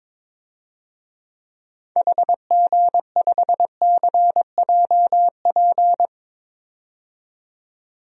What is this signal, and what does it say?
Morse code, heard as a keyed tone, about 23 wpm, 700 Hz. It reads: HG5CJP
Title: DXCC-100 Call Signs - 6 Characters - 22wpm